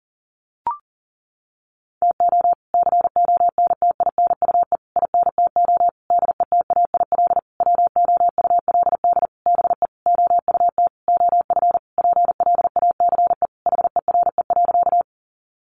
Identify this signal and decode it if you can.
Morse code, heard as a keyed tone, about 40 wpm, 700 Hz. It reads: TO CONTINUE INTO DETAIL WOULD BE OUT OF PLACE HERE.
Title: Sentences from Top 1500 Words 40wpm